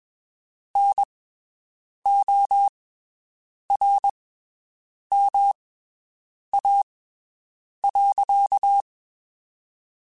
Morse code, heard as a keyed tone, about 22 wpm, spaced as for 7 wpm, 800 Hz. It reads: NORMA.
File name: (spaced) / QSOs Spaced-mp3s / WD0FBT 1 @20 sp